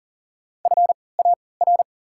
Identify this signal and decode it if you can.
Morse code, heard as a keyed tone, about 40 wpm, spaced as for 23 wpm, 700 Hz. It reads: FAR
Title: 3x Character Spacing - Top 500 Words - 40wpm